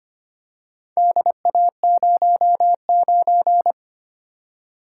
Morse code, heard as a keyed tone, about 25 wpm, 700 Hz. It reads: DA09
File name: practice / 25wpm / Four Letter-Number Combinations - 25wpm